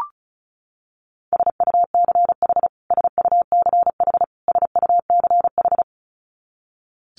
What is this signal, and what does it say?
Morse code, heard as a keyed tone, about 34 wpm, 700 Hz. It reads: SUCH SUCH SUCH
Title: Top 200 Words Repeated 35wpm